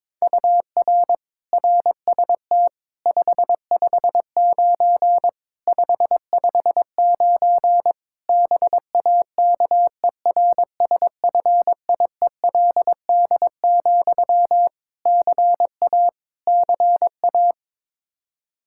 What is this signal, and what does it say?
Morse code, heard as a keyed tone, about 21 wpm, 700 Hz. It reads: UR RST 559 559 BAKERSFIELD, CA CA
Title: K1USN SST QSO Practice 22wpm